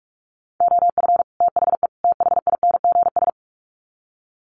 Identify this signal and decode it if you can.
Morse code, heard as a keyed tone, about 44 wpm, 700 Hz. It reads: OF THE THINGS
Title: Sets of 3 Words 45wpm